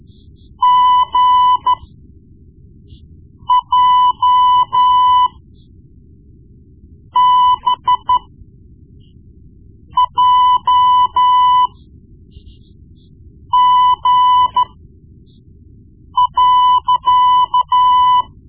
Morse code, heard as a keyed tone, about 13 wpm, spaced as for 4 wpm, 1000 Hz. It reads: GJBJG.
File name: Morse Lessons / Morse Lessons / Tape-1B